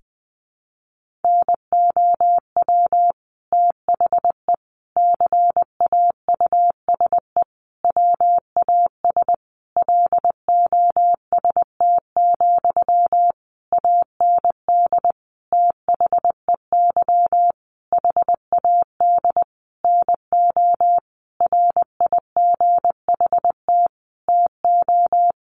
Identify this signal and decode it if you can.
Morse code, heard as a keyed tone, about 20 wpm, 700 Hz. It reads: NOW THE CAUSE WAS LOST, AND THEY HAD NO RIGHT TO
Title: Sentences from Top 1500 Words 20wpm